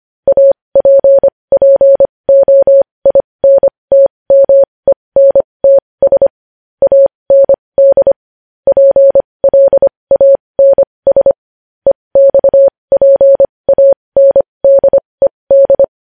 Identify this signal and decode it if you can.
Morse code, heard as a keyed tone, about 25 wpm, spaced as for 20 wpm, 550 Hz. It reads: APPOINTMENTS AND PLANS EXPANDED
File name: podcast / News headlines-1x-25x20